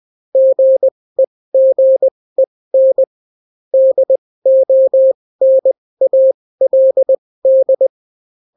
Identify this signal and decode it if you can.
Morse code, heard as a keyed tone, about 20 wpm, spaced as for 16 wpm, 530 Hz. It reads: GEGEN DONALD